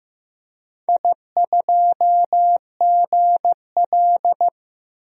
Morse code, heard as a keyed tone, about 15 wpm, 700 Hz. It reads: I2GL
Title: DXCC-100 Call Signs - 4 Characters - 15wpm